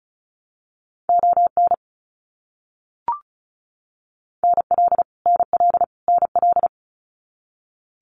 Morse code, heard as a keyed tone, about 34 wpm, 700 Hz. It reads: ON NL NL NL
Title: Canadian Provinces and Territories - Repeated 3x - 35wpm